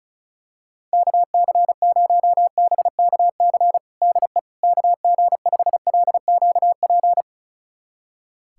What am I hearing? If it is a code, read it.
Morse code, heard as a keyed tone, about 36 wpm, 700 Hz. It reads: KC0BKC DE KG5LQP
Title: USA Callsign DE Callsign - 6 Characters - 35wpm